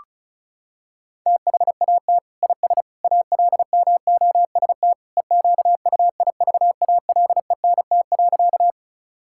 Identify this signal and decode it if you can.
Morse code, heard as a keyed tone, about 36 wpm, 700 Hz. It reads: THAT IS ALMOST EQUIVALENT.
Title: Sentences from Top 1600 Words 35wpm